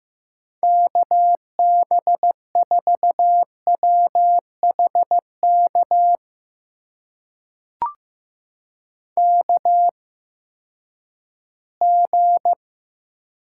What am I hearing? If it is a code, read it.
Morse code, heard as a keyed tone, about 15 wpm, spaced as for 10 wpm, 700 Hz. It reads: KB4WHK KG
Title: USA Call Signs - 6 Characters - 8x Character Spacing - 15wpm